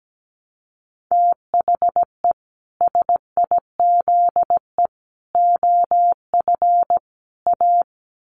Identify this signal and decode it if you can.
Morse code, heard as a keyed tone, about 17 wpm, 700 Hz. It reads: THE SIZE OF A
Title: Sets of 4 Words 17wpm